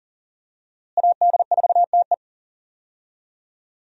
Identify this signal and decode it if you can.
Morse code, heard as a keyed tone, about 40 wpm, 700 Hz. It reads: AD4TE
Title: USA Call Signs - 5 Characters - Repeated 2x - 40wpm